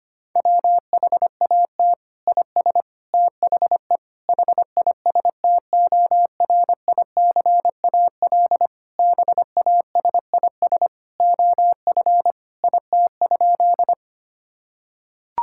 Morse code, heard as a keyed tone, about 25 wpm, 700 Hz. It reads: WHAT IS THE HISTORICAL BASIS OF IT?
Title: Sentences from Top 1000 Words - Review Part 5 - 25wpm